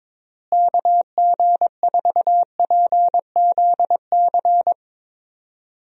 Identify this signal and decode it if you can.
Morse code, heard as a keyed tone, about 21 wpm, 700 Hz. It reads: KG4PZC
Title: USA Call Signs - 6 Characters - Repeated 3x - 22wpm